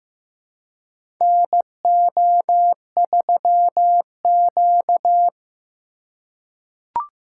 Morse code, heard as a keyed tone, about 15 wpm, 700 Hz. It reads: NO3Q
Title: USA Call Signs - 4 Characters - 3x Character Spacing - 15wpm